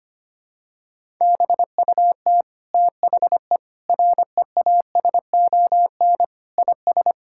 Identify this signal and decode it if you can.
Morse code, heard as a keyed tone, about 25 wpm, 700 Hz. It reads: BUT THE REASON IS
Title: Sentences from Top 1600 Words 25wpm